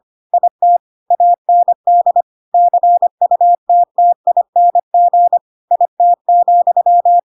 Morse code, heard as a keyed tone, about 25 wpm, 700 Hz. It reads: IT AND CUTTING IT,
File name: Pinocchio / Pinocchio ch3 25 25 ik1hge